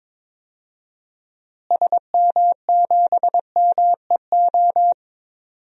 Morse code, heard as a keyed tone, about 22 wpm, 700 Hz. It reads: SM7MEO
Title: DXCC-100 Call Signs - 6 Characters - Repeated 3x - 22wpm